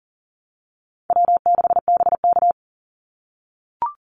Morse code, heard as a keyed tone, about 40 wpm, 700 Hz. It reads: W6BK
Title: Call Signs - 40wpm